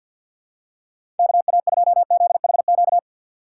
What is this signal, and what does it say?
Morse code, heard as a keyed tone, about 51 wpm, 700 Hz. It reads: KA2ZHQ
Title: USA Call Signs - 3x Character Spacing - 50wpm